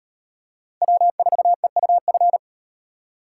Morse code, heard as a keyed tone, about 38 wpm, 700 Hz. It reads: W4EUF